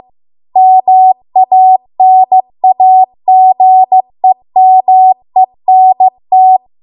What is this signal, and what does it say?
Morse code, heard as a keyed tone, about 15 wpm, 750 Hz. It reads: MANAGEMENT